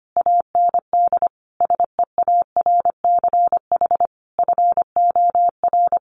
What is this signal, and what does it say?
Morse code, heard as a keyed tone, about 25 wpm, 700 Hz. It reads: AND SEARCH FOR
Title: Sentences from Top 800 Words 25wpm